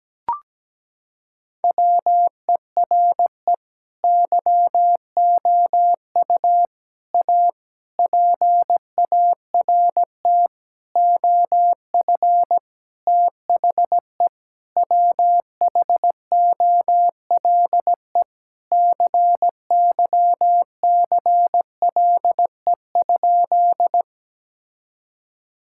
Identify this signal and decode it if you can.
Morse code, heard as a keyed tone, about 17 wpm, 700 Hz. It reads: WERE YOU A PART OF THE WHOLE CYCLE?
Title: Sentences from Top 1000 Words - Review Part 6 - 17wpm